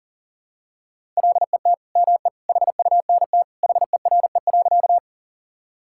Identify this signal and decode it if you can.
Morse code, heard as a keyed tone, about 40 wpm, 700 Hz. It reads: LET ME HUNT HERE.
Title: Sentences from Top 1600 Words 40wpm